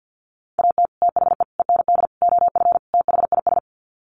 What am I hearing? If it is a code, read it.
Morse code, heard as a keyed tone, about 48 wpm, 700 Hz. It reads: AT THE END OF THIS
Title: Sets of 5 Words 50wpm